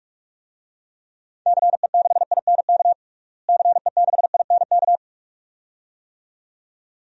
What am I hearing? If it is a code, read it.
Morse code, heard as a keyed tone, about 46 wpm, 700 Hz. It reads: CE6INK CE6INK
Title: DXCC-100 Call Signs - 6 Characters - Repeated 2x - 45wpm